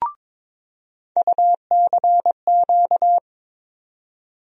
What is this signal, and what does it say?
Morse code, heard as a keyed tone, about 22 wpm, 700 Hz. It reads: UCQ
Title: USA Call Sign Postfixes - 3 Letters 22wpm